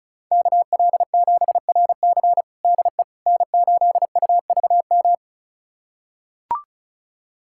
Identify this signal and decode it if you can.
Morse code, heard as a keyed tone, about 36 wpm, 700 Hz. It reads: KL7RC DE N8UVM
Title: USA Callsign DE Callsign - 5 Characters - 35wpm